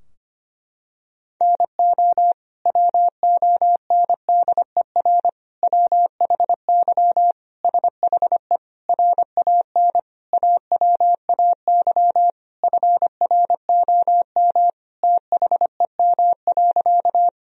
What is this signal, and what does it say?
Morse code, heard as a keyed tone, about 25 wpm, 700 Hz. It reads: NO WONDER WHY SHE RAN AWAY FROM THEM.
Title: Sentences from Top 1000 Words - Review Part 3 - 25wpm